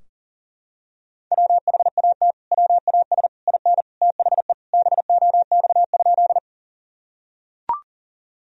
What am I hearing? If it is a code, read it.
Morse code, heard as a keyed tone, about 40 wpm, 700 Hz. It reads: WHAT WAS IN THE BOX?
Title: Sentences from Top 1000 Words - Review Part 2 - 40wpm